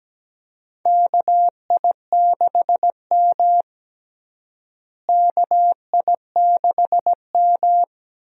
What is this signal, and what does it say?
Morse code, heard as a keyed tone, about 17 wpm, 700 Hz. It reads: KI6M KI6M
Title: USA Call Signs - 4 Characters - Repeated 2x - 17wpm